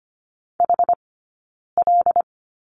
Morse code, heard as a keyed tone, about 25 wpm, 700 Hz. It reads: H L
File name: course / 25:10wpm / 092 Lesson 16d - Letter c - Two Characters 25wpm